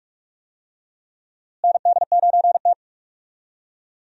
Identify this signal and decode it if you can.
Morse code, heard as a keyed tone, about 43 wpm, 700 Hz. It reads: ND9T
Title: USA Call Signs - 4 Characters - Repeated 2x - 45wpm